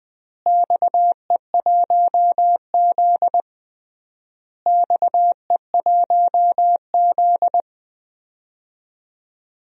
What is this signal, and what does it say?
Morse code, heard as a keyed tone, about 20 wpm, 700 Hz. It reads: XE1Z XE1Z
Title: DXCC-100 Call Signs - 4 Characters - Repeated 3x - 20wpm